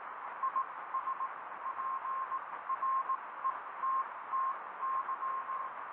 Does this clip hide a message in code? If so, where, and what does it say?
Yes: Morse code from 0.4 to 5.7 s, heard as a keyed tone, about 18 wpm, spaced as for 15 wpm, 1060 Hz, with radio noise. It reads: IS PRETTY